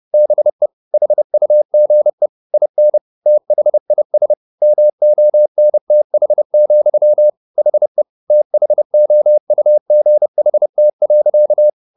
Morse code, heard as a keyed tone, about 30 wpm, 600 Hz. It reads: BE HUGE IN THIS MONTH, HE THOUGHT.
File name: the old man and the sea / the old man and the sea 1 30wpm